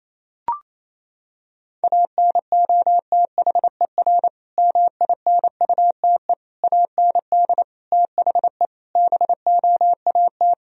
Morse code, heard as a keyed tone, about 28 wpm, 690 Hz. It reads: ANOTHER MINUTE AND THE BOAT
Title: Sentences from Top 1000 Words - Review Part 3 - 28wpm